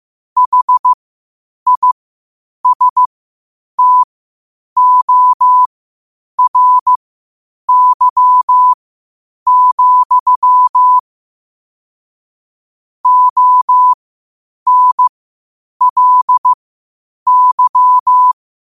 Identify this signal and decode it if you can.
Morse code, heard as a keyed tone, about 15 wpm, spaced as for 8 wpm, 1000 Hz. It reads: HISTORY, ONLY
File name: qotdfiles / 8wpm